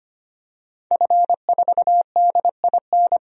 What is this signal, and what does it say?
Morse code, heard as a keyed tone, about 25 wpm, 700 Hz. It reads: F4DIN
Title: DXCC-100 Call Signs - 5 Characters - 25wpm